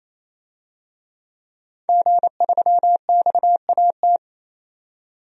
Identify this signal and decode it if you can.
Morse code, heard as a keyed tone, about 28 wpm, 700 Hz. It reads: G3XAT
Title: DXCC-100 Call Signs - 5 Characters - 28wpm